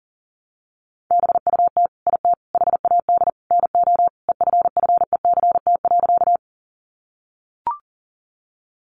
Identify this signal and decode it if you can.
Morse code, heard as a keyed tone, about 40 wpm, 700 Hz. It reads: BUT IT HAD NO EFFECT.
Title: Sentences from Top 1000 Words 40wpm